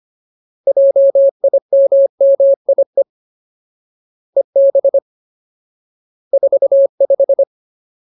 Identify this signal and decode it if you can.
Morse code, heard as a keyed tone, about 25 wpm, 550 Hz. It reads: JIMMIE EB 45
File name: Ultimate Word List / 25WPM Ultimate